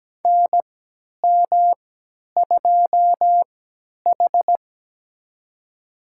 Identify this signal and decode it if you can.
Morse code, heard as a keyed tone, about 17 wpm, spaced as for 10 wpm, 700 Hz. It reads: NM2H
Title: USA Call Signs - 4 Characters - 3x Character Spacing - 17wpm